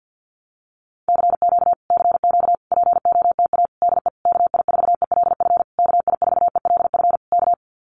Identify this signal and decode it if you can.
Morse code, heard as a keyed tone, about 50 wpm, 700 Hz. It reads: CQ CQ POTA DE KI4ELF KI4ELF K
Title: Common QSO Phrases Encore 50wpm